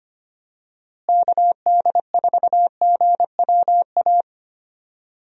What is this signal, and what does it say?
Morse code, heard as a keyed tone, about 25 wpm, 700 Hz. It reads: KD4GWA
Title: USA Call Signs - 6 Characters - Repeated 2x - 25wpm